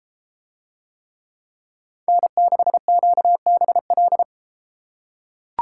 Morse code, heard as a keyed tone, about 33 wpm, 700 Hz. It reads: N6QBL